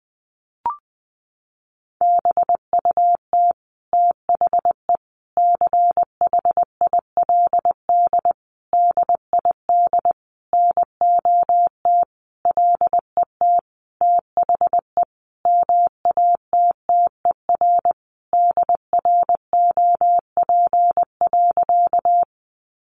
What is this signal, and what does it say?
Morse code, heard as a keyed tone, about 20 wpm, 700 Hz. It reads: BUT THE CHILD DID NOT LET THE MATTER DROP.
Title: Sentences from Top 1400 Words 20wpm